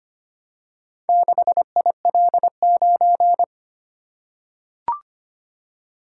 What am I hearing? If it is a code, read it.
Morse code, heard as a keyed tone, about 25 wpm, 700 Hz. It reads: 6IL9
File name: practice / 25wpm / Four Letter-Number Combinations - 25wpm